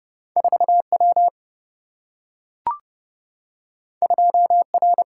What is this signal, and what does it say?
Morse code, heard as a keyed tone, about 30 wpm, 700 Hz. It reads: 4 W 2 R
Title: Single-Letter Single-Number 30wpm